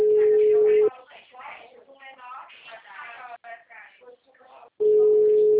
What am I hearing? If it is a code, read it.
Ringback tone (Europe, 425 Hz), with speech.